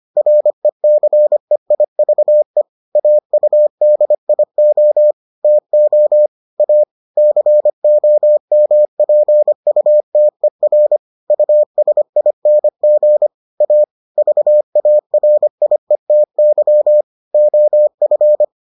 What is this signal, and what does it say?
Morse code, heard as a keyed tone, about 25 wpm, 600 Hz. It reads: RECEIVE AUDIO TO A COMPUTER USING A VARIETY OF